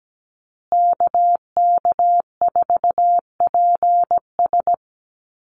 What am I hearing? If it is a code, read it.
Morse code, heard as a keyed tone, about 17 wpm, 700 Hz. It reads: KK4PS
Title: USA Call Signs Repeated 3x 17wpm